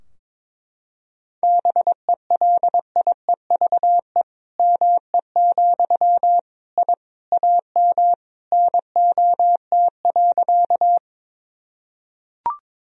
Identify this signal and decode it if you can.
Morse code, heard as a keyed tone, about 22 wpm, 700 Hz. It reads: BELIEVE ME, I AM NOT.
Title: Sentences from Top 1000 Words - Review Part 2 - 22wpm